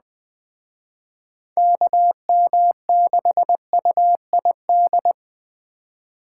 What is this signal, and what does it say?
Morse code, heard as a keyed tone, about 20 wpm, 700 Hz. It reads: KM6UID